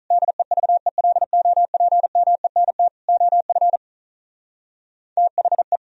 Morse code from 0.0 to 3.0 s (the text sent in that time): DEVELOPMENT